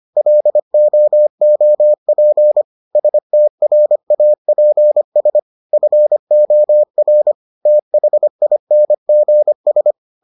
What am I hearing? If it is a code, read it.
Morse code, heard as a keyed tone, about 25 wpm, 600 Hz. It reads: LOOP STRAPS FOR THINGS